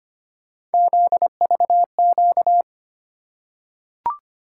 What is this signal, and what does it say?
Morse code, heard as a keyed tone, about 25 wpm, 700 Hz. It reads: ZVQ
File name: Three Letter-Number Combinations - 25wpm